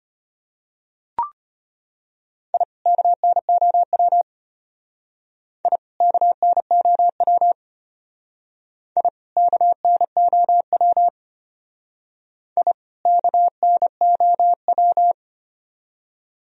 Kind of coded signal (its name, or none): Morse code